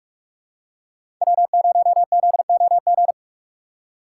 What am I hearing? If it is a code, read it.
Morse code, heard as a keyed tone, about 41 wpm, 700 Hz. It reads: W0ZOG
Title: USA Call Signs - 5 Characters - 3x Character Spacing - 45wpm